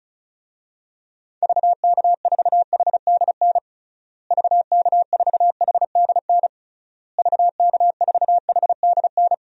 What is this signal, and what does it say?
Morse code, heard as a keyed tone, about 36 wpm, 700 Hz. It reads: VK4HDN VK4HDN VK4HDN
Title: DXCC-100 Call Signs - 6 Characters - Repeated 3x - 35wpm